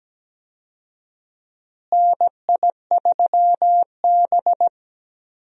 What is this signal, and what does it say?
Morse code, heard as a keyed tone, about 17 wpm, 700 Hz. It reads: NI3B
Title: USA Call Signs - 4 Characters - Repeated 2x - 17wpm